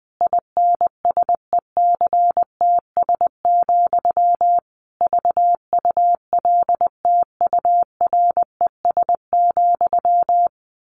Morse code, heard as a keyed tone, about 19 wpm, 700 Hz. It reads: INSECTS, VULTURES,
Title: A Sound Of Thunder 20wpm